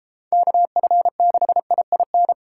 Morse code, heard as a keyed tone, about 30 wpm, 700 Hz. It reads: KF6IIN